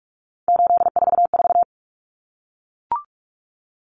Morse code, heard as a keyed tone, about 44 wpm, 700 Hz. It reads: 844
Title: Common Cut Numbers - 3-Digit 45wpm